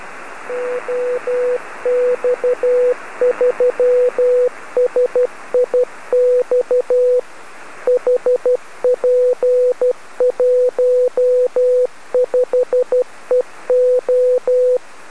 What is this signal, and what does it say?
Morse code, heard as a keyed tone, about 12 wpm, 500 Hz. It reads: OX3SIX HP15EO
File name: OX3SIX B